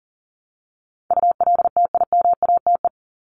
Morse code, heard as a keyed tone, about 40 wpm, 700 Hz. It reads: ULTIMATE
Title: LY Suffix - Top 100 Words with Base Words 40wpm